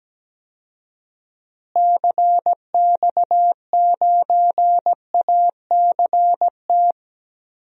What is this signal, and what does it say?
Morse code, heard as a keyed tone, about 17 wpm, 700 Hz. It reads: CX9ACT
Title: DXCC-100 Call Signs - 6 Characters - Repeated 2x - 17wpm